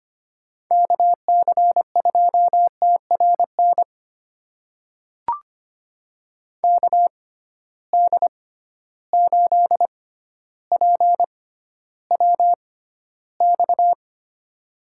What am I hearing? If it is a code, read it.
Morse code, heard as a keyed tone, about 25 wpm, spaced as for 13 wpm, 700 Hz. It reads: KC2TRN KD8PWX
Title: USA Call Signs - 6 Characters - 6x Character Spacing - 25wpm